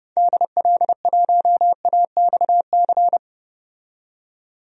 Morse code, heard as a keyed tone, about 30 wpm, 700 Hz. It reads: DL1AXC
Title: DXCC-100 Call Signs - 6 Characters - 30wpm